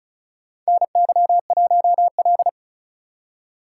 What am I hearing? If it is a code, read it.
Morse code, heard as a keyed tone, about 34 wpm, 700 Hz. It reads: NY1L
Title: USA Call Signs - 4 Characters - 6x Character Spacing - 35wpm